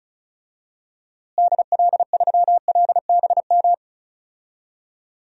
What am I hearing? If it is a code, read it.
Morse code, heard as a keyed tone, about 36 wpm, 700 Hz. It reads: DL3LBM
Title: DXCC-100 Call Signs - 6 Characters - Repeated 3x - 35wpm